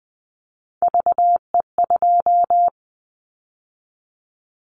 Morse code, heard as a keyed tone, about 20 wpm, 700 Hz. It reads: VE2
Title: Callsign Prefixes for Top 100 DXCC - 20wpm